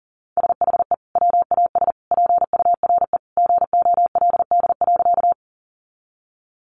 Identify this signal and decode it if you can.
Morse code, heard as a keyed tone, about 40 wpm, 700 Hz. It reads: SHE WAS PURE GOLD.
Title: Sentences from Top 1500 Words 40wpm